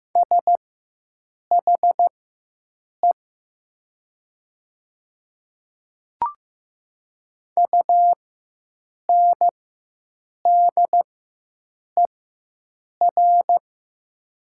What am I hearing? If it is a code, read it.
Morse code, heard as a keyed tone, about 15 wpm, spaced as for 7 wpm, 690 Hz. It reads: SHE UNDER
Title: 4x Character Spacing - Top 200 Words - 15wpm